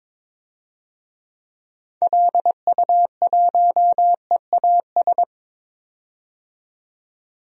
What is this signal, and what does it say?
Morse code, heard as a keyed tone, about 22 wpm, 700 Hz. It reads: LU1EAS
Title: DXCC-100 Call Signs - 6 Characters - 22wpm